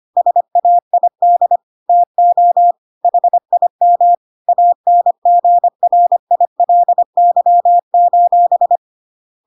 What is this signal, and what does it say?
Morse code, heard as a keyed tone, about 25 wpm, 700 Hz. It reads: SAID TO HIM ANGRILY:
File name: Pinocchio / Pinocchio ch2 25 25 ik1hge